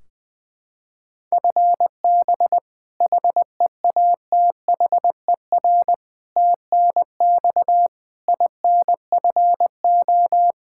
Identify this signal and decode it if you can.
Morse code, heard as a keyed tone, about 20 wpm, 700 Hz. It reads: FB HEATHER TNX INFO
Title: Practice QSO - Exchange 3 - More Info - 20wpm